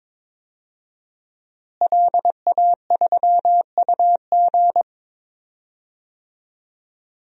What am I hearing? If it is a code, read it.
Morse code, heard as a keyed tone, about 22 wpm, 700 Hz. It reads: LA3UG
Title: DXCC-100 Call Signs - 5 Characters - Repeated 2x - 22wpm